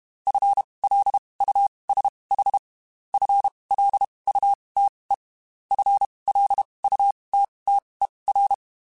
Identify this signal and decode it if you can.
Morse code, heard as a keyed tone, about 32 wpm, spaced as for 23 wpm, 800 Hz. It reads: FLUSH FLUTE FLUTTER
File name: Spaced / FL Runs sp mp3s / FL Runs sp @30